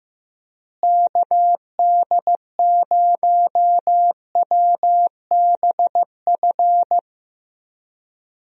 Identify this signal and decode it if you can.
Morse code, heard as a keyed tone, about 15 wpm, 700 Hz. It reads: KD0WBF
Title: USA Call Signs - 6 Characters - Repeated 3x - 15wpm